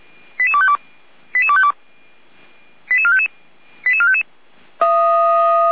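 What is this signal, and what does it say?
Five-tone selective call (ZVEI), heard as quick runs of five tones, tones of about 70 ms. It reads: 88252 88252 88455 88455
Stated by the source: ZVEI five-tone selective call received on SDR.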